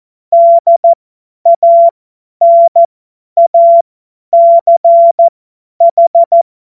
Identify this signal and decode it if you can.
Morse code, heard as a keyed tone, about 14 wpm, spaced as for 10 wpm, 680 Hz. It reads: DANACH
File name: Morsekurs / Witze / Witz 10. Mai 2022 50 F70